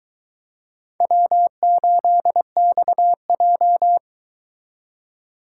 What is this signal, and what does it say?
Morse code, heard as a keyed tone, about 24 wpm, 700 Hz. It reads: W8XJ